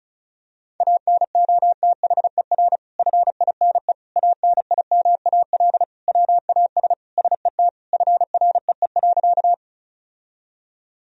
Morse code, heard as a keyed tone, about 36 wpm, 700 Hz. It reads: ANOTHER FINE ANIMAL WAS SET FREE.
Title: Sentences from Top 1000 Words - Review Part 1 - 35wpm